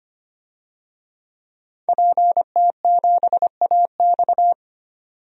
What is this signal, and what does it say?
Morse code, heard as a keyed tone, about 25 wpm, 700 Hz. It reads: PT7AX
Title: DXCC-100 Call Signs - 5 Characters - 25wpm